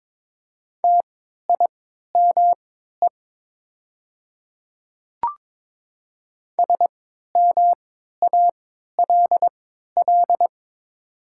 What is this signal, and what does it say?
Morse code, heard as a keyed tone, about 22 wpm, spaced as for 12 wpm, 700 Hz. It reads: TIME SMALL
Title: 3x Character Spacing - Top 200 Words - 22wpm